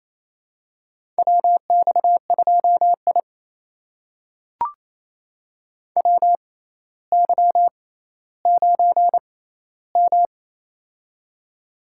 Morse code, heard as a keyed tone, about 28 wpm, spaced as for 14 wpm, 700 Hz. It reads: WX2I WY9M
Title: USA Call Signs - 4 Characters - 6x Character Spacing - 28wpm